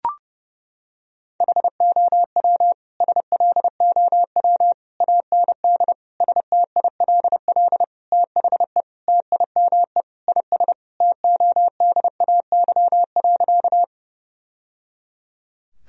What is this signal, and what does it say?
Morse code, heard as a keyed tone, about 32 wpm, 700 Hz. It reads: HOW SLOW AND STILL THE TIME IS TODAY.
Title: Sentences from Top 1000 Words - Review Part 2 - 30wpm